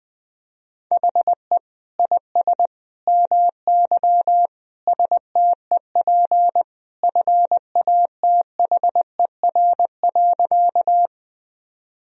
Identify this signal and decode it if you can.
Morse code, heard as a keyed tone, about 21 wpm, 700 Hz. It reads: HE IS MY STEP FATHER.
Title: Sentences from Top 1000 Words - Review Part 2 - 20wpm